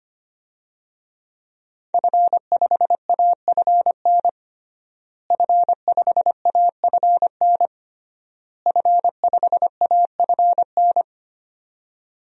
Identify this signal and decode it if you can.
Morse code, heard as a keyed tone, about 25 wpm, 700 Hz. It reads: F5AFN F5AFN F5AFN
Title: DXCC-100 Call Signs - 5 Characters - Repeated 3x - 25wpm